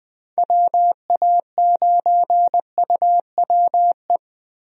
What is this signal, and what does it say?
Morse code, heard as a keyed tone, about 20 wpm, 700 Hz. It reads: WA9UWE
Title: USA Call Signs - 6 Characters - Repeated 3x - 20wpm